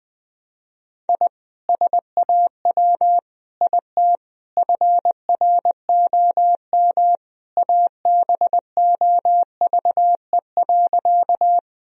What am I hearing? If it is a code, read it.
Morse code, heard as a keyed tone, about 20 wpm, 700 Hz. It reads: I SAW IT FROM ABOVE.
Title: Sentences from Top 1000 Words - Review Part 2 - 20wpm